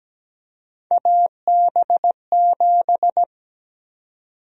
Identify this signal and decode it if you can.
Morse code, heard as a keyed tone, about 17 wpm, 700 Hz. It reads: AB7
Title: USA Call Sign Prefixes - Repeated 2x - 17wpm